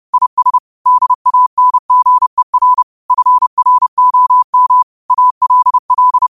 Morse code, heard as a keyed tone, about 30 wpm, 1000 Hz. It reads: IS DANGER FROM ALL
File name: qotdfiles / 30wpm